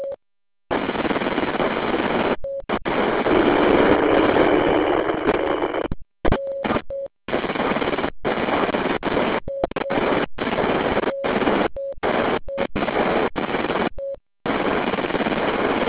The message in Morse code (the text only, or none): none